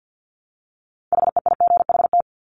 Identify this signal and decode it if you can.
Morse code, heard as a keyed tone, about 50 wpm, 700 Hz. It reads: HEIGHT
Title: Multi-letter Phonemes Repeated 3x 50wpm